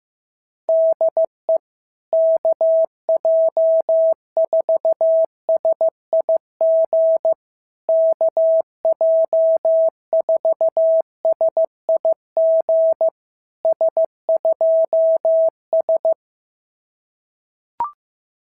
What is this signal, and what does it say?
Morse code, heard as a keyed tone, about 14 wpm, 650 Hz. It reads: DE KJ4SIG KJ4SIG S2S
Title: SOTA QSO Practice 15wpm